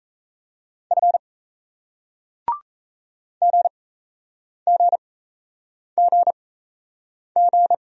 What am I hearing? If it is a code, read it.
Morse code, heard as a keyed tone, about 38 wpm, spaced as for 8 wpm, 700 Hz. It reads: F GGGG